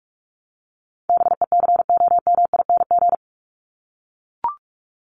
Morse code, heard as a keyed tone, about 44 wpm, 700 Hz. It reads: BECOMING
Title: ING Suffix - Top 100 Words with Base Words 45wpm